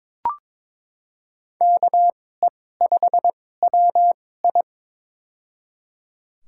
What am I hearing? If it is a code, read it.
Morse code, heard as a keyed tone, about 22 wpm, spaced as for 16 wpm, 700 Hz. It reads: KE5WI
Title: USA Call Signs - 5 Characters - 2x Character Spacing - 22wpm